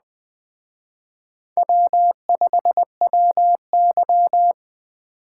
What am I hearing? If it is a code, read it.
Morse code, heard as a keyed tone, about 23 wpm, 700 Hz. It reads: W5WY
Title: Speed Racing - USA 1x2 Call Signs 20wpm